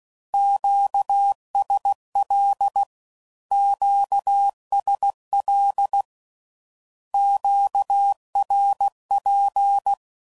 Morse code, heard as a keyed tone, about 16 wpm, 800 Hz. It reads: QSL QSL QRP
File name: CW Code Groups @15WPM / Q Calls / Q Calls Twice